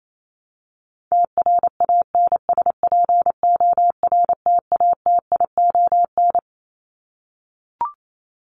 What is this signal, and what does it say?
Morse code, heard as a keyed tone, about 28 wpm, 700 Hz. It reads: TRANSPORTATION
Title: Top 1501-1750 Words 28wpm